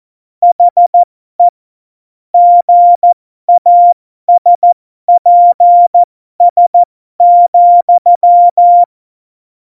Morse code, heard as a keyed tone, about 14 wpm, spaced as for 12 wpm, 700 Hz. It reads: HE GASPS,
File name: Morsekurs / Witze